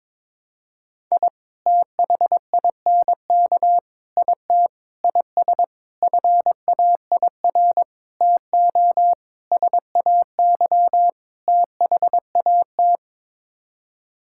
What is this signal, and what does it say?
Morse code, heard as a keyed tone, about 22 wpm, 700 Hz. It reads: I THINK IT IS FAIR TO SAY THAT
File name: practice / 22wpm / Sets of 8 Words - 22wpm